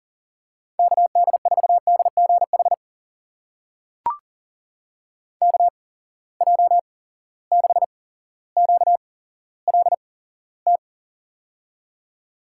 Morse code, heard as a keyed tone, about 41 wpm, spaced as for 17 wpm, 700 Hz. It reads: KD4DGH KJ6QLT
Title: USA Call Signs - 6 Characters - 8x Character Spacing - 40wpm